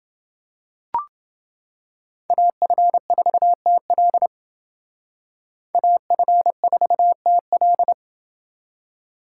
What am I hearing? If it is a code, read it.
Morse code, heard as a keyed tone, about 29 wpm, 700 Hz. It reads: AF4TL AF4TL